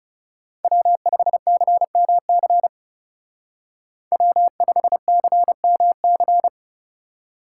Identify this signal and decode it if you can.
Morse code, heard as a keyed tone, about 37 wpm, 700 Hz. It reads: W5CMC W5CMC
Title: Speed Racing - USA Call Signs 30wpm